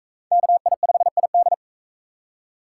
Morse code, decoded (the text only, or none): KI5ID